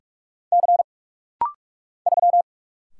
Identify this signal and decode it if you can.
Morse code, heard as a keyed tone, about 45 wpm, 700 Hz. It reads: C 3
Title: Single Letter-Number Alt - Rapid-Fire 45wpm